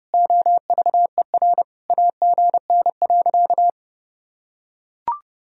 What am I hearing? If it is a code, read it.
Morse code, heard as a keyed tone, about 30 wpm, 700 Hz. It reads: OVER AGN.
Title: Sentences from Top 500 Words and Abbreviations 30wpm